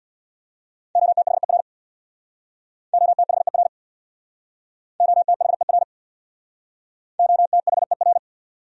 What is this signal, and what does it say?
Morse code, heard as a keyed tone, about 66 wpm, 700 Hz. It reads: OTHER OTHER OTHER OTHER